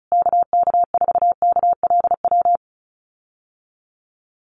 Morse code, heard as a keyed tone, about 34 wpm, 700 Hz. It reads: KK4KLW
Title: USA Call Signs Repeated 3x 35wpm